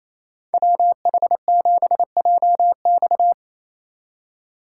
Morse code, heard as a keyed tone, about 36 wpm, 700 Hz. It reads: WH7JX